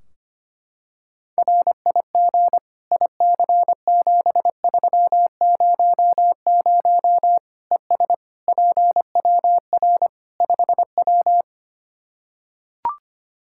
Morse code, heard as a keyed tone, about 25 wpm, 700 Hz. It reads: RIG IC7300 ES PWR 5W
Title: Practice QSO - Exchange 2 - Info - 25wpm